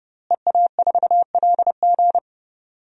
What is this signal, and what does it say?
Morse code, heard as a keyed tone, about 32 wpm, 700 Hz. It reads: EA4LG
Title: DXCC-100 Call Signs - 5 Characters - Repeated 3x - 30wpm